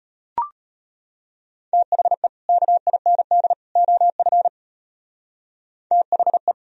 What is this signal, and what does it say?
Morse code, heard as a keyed tone, about 31 wpm, 700 Hz. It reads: THE KIND OF THE